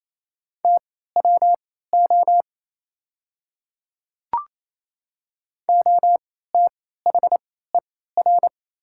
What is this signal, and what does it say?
Morse code, heard as a keyed tone, about 28 wpm, spaced as for 16 wpm, 700 Hz. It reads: TWO OTHER
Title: 3x Character Spacing - Top 100 Words - 28wpm